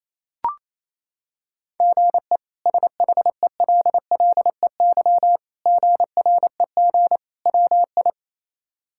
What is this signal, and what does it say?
Morse code, heard as a keyed tone, about 28 wpm, 700 Hz. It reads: GE SHELLEY GREG WI
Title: K1USN SST QSO Practice - 28wpm